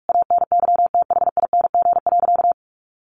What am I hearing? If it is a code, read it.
Morse code, heard as a keyed tone, about 44 wpm, 700 Hz. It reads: ANYTHING.
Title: Sentences from Top 1500 Words 45wpm